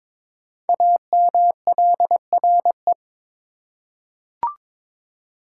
Morse code, heard as a keyed tone, about 22 wpm, 700 Hz. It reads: AMLRE
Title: Code Groups - 5 Letters - 22wpm